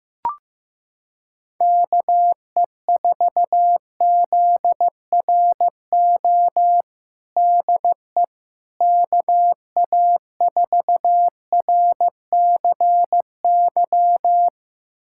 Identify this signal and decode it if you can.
Morse code, heard as a keyed tone, about 15 wpm, 700 Hz. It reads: KE4ZRO DE KA4RCY
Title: USA Callsign DE Callsign 15wpm